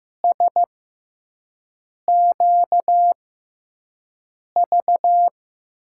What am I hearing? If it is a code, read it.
Morse code, heard as a keyed tone, about 15 wpm, spaced as for 5 wpm, 700 Hz. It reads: SQV
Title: USA Call Signs - 5 Characters - 6x Character Spacing - 15wpm